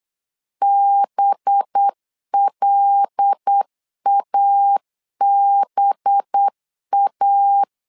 Morse code, heard as a keyed tone, about 8 wpm, 800 Hz. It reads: BLABA
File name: LEZIONI / 5 LBVC / 5 LB 8wpm 2816